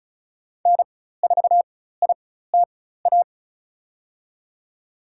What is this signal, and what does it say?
Morse code, heard as a keyed tone, about 36 wpm, spaced as for 16 wpm, 700 Hz. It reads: N4ITA
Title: USA Call Signs - 5 Characters - 4x Character Spacing - 35wpm